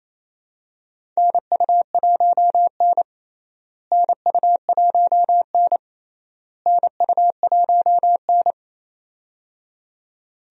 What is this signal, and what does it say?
Morse code, heard as a keyed tone, about 28 wpm, 700 Hz. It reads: NU1N NU1N NU1N
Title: USA Call Signs - 4 Characters - Repeated 3x - 28wpm